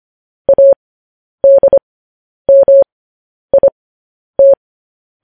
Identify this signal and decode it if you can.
Morse code, heard as a keyed tone, about 25 wpm, spaced as for 10 wpm, 550 Hz. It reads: ADMIT
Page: news headlines in morse code at 10 wpm